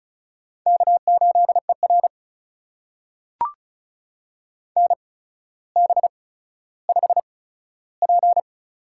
Morse code, heard as a keyed tone, about 36 wpm, spaced as for 15 wpm, 700 Hz. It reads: K8ER NB5P
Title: USA Call Signs - 4 Characters - 8x Character Spacing - 35wpm